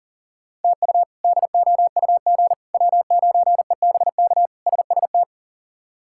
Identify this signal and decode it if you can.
Morse code, heard as a keyed tone, about 40 wpm, 700 Hz. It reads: TU DOUG W9EBK SST
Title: K1USN SST QSO Practice - 40wpm